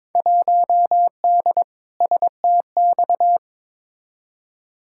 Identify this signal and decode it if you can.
Morse code, heard as a keyed tone, about 22 wpm, 700 Hz. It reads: 1D STX
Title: Field Day Exchanges - 22wpm